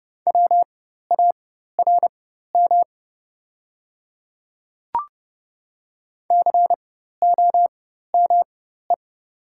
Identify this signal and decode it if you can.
Morse code, heard as a keyed tone, about 30 wpm, spaced as for 14 wpm, 700 Hz. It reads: WARM COME
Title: 4x Character Spacing - Top 500 Words - 30wpm